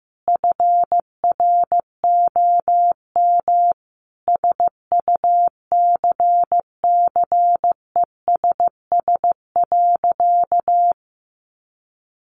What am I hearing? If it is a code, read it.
Morse code, heard as a keyed tone, about 15 wpm, 700 Hz. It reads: FROM SUCCESS.
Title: Sentences from Top 900 Words 15wpm